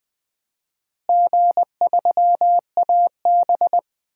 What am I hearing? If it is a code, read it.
Morse code, heard as a keyed tone, about 21 wpm, 700 Hz. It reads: G3AB
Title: DXCC-100 Call Signs - 4 Characters - Repeated 3x - 20wpm